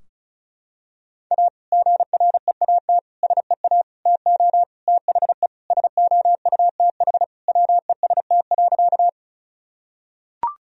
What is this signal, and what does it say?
Morse code, heard as a keyed tone, about 36 wpm, 700 Hz. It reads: A GREAT SEA TO THE SOUTH WEST.
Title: Sentences from Top 1000 Words - Review Part 2 - 35wpm